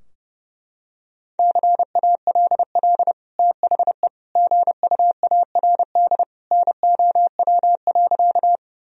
Morse code, heard as a keyed tone, about 32 wpm, 700 Hz. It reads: CALL THE GUARD NOW.
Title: Sentences from Top 1000 Words - Review Part 4 - 30wpm